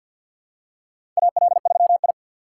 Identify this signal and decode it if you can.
Morse code, heard as a keyed tone, about 46 wpm, 700 Hz. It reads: AL3I
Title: USA Call Signs - 6x Character Spacing - 50wpm